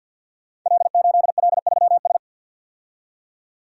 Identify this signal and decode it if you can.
Morse code, heard as a keyed tone, about 52 wpm, 700 Hz. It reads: L8L3S
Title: Code Groups - 5 Letter:Numbers - 50wpm